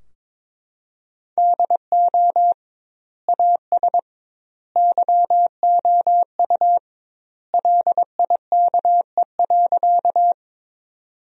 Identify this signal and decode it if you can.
Morse code, heard as a keyed tone, about 22 wpm, 700 Hz. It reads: DO AS YOU LIKE.
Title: 2x Word-Spacing - Sentences from Top 100 Words - 22wpm